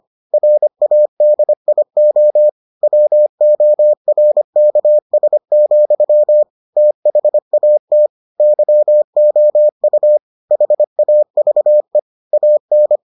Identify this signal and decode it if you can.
Morse code, heard as a keyed tone, about 25 wpm, 600 Hz. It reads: RADIO WORKS, THAT YOU HAVE AN